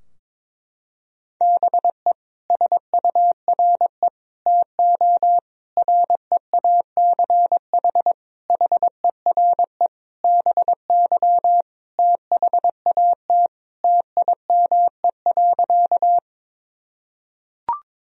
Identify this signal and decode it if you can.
Morse code, heard as a keyed tone, about 22 wpm, 700 Hz. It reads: BE SURE TO REACH HERE BY THAT TIME.
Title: Sentences from Top 1000 Words - Review Part 2 - 22wpm